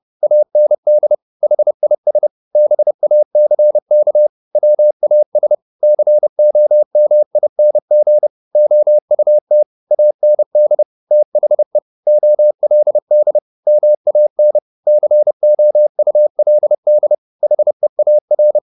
Morse code, heard as a keyed tone, about 30 wpm, 600 Hz. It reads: AND HIS BACK WAS COMING OUT AND THE OLD MAN COULD HEAR